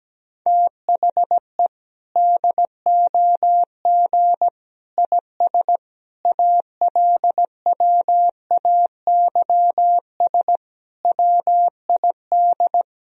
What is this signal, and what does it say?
Morse code, heard as a keyed tone, about 17 wpm, 700 Hz. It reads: THE DOG IS ALWAYS WID
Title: Sentences from Top 400 Words and Abbreviations 17wpm